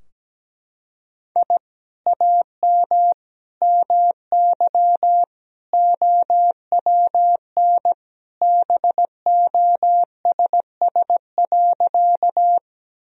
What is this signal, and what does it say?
Morse code, heard as a keyed tone, about 17 wpm, 700 Hz. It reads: I AM MY OWN BOSS.
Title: Sentences from Top 1000 Words - Review Part 4 - 17wpm